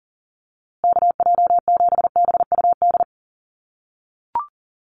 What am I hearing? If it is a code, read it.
Morse code, heard as a keyed tone, about 40 wpm, 700 Hz. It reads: KJ7BUD
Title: USA Call Signs Repeated 2x 40wpm